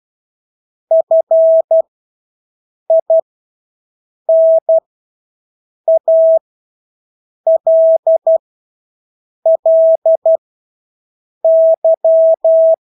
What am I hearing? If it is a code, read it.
Morse code, heard as a keyed tone, about 12 wpm, spaced as for 6 wpm, 650 Hz. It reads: FINALLY
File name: Engelske ord 12-6 wpm